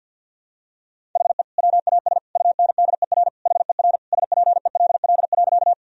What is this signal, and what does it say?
Morse code, heard as a keyed tone, about 51 wpm, 700 Hz. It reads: HE WAS UNDER HER SPELL.
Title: Sentences from Top 1000 Words - Review Part 1 - 50wpm